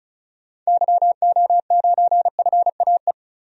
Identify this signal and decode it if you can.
Morse code, heard as a keyed tone, about 36 wpm, 700 Hz. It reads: YO9FAE
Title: DXCC-100 Call Signs - 6 Characters - 35wpm